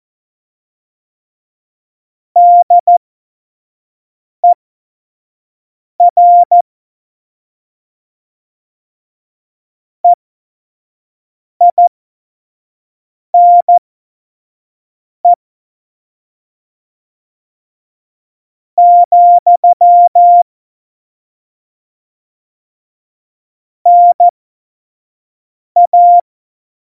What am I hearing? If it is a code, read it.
Morse code, heard as a keyed tone, about 14 wpm, spaced as for 5 wpm, 700 Hz. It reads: DER EINE , NA